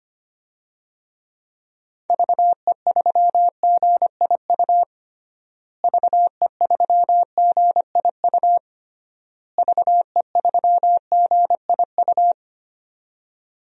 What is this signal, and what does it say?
Morse code, heard as a keyed tone, about 25 wpm, 700 Hz. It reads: VE3GIU VE3GIU VE3GIU
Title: DXCC-100 Call Signs - 6 Characters - Repeated 3x - 25wpm